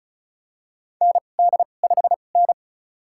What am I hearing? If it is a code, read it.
Morse code, heard as a keyed tone, about 36 wpm, spaced as for 25 wpm, 700 Hz. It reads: ND5N
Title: USA Call Signs - 2x Character Spacing - 35wpm